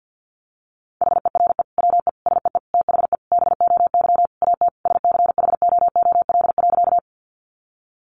Morse code, heard as a keyed tone, about 50 wpm, 700 Hz. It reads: HERE WE SEE THE BOY AT SCHOOL.
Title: Sentences from Top 300 Words 50wpm